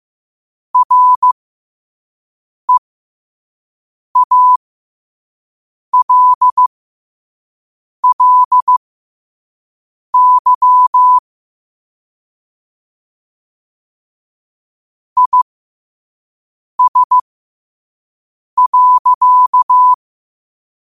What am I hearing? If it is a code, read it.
Morse code, heard as a keyed tone, about 15 wpm, spaced as for 5 wpm, 1000 Hz. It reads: REALLY IS.
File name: qotdfiles / 5wpm